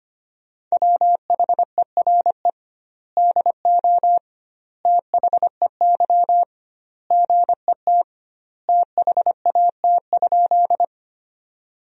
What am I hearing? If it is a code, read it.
Morse code, heard as a keyed tone, about 25 wpm, 700 Hz. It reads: WHERE DO THEY GET THAT?
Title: 2x Word-Spacing - Sentences from Top 200 Words - 25wpm